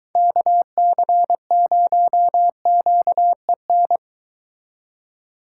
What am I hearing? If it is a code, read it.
Morse code, heard as a keyed tone, about 22 wpm, 700 Hz. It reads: KC0QEN